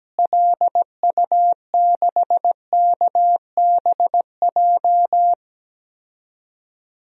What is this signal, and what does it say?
Morse code, heard as a keyed tone, about 17 wpm, 700 Hz. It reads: LU6KBJ
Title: DXCC-100 Call Signs - 6 Characters - Repeated 3x - 17wpm